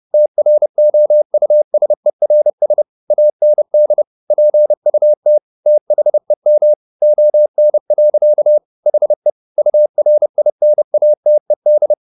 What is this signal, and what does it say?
Morse code, heard as a keyed tone, about 30 wpm, 600 Hz. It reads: TROUSERS AND PUT THEM ON. HE URINATED
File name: the old man and the sea / the old man and the sea 1 30wpm